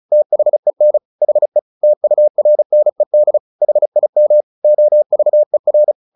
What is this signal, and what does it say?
Morse code, heard as a keyed tone, about 34 wpm, 600 Hz. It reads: THEN HE TURNED HIM OVER
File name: the old man and the sea 2 35wpm